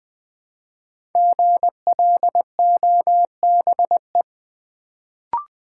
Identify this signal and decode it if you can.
Morse code, heard as a keyed tone, about 20 wpm, 700 Hz. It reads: GLOBE
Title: Build A Word - 1 Letter At A Time - 5-Letter Words - 20wpm